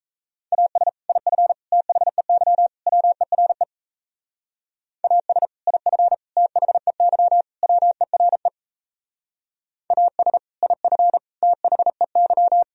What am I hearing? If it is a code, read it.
Morse code, heard as a keyed tone, about 34 wpm, 700 Hz. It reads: AS IF THEY WERE AS IF THEY WERE AS IF THEY
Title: Speed-Racing - Sets of 4 Words 28wpm